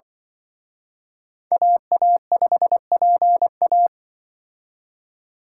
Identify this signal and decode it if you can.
Morse code, heard as a keyed tone, about 24 wpm, 700 Hz. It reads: AA5PA
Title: Speed Racing - USA 2x2 Call Signs 20wpm